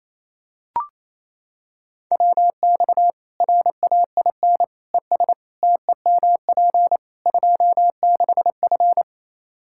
Morse code, heard as a keyed tone, about 28 wpm, 700 Hz. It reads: WX RAIN ES TEMP 26F
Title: Practice QSO - Exchange 2 - Info - 28wpm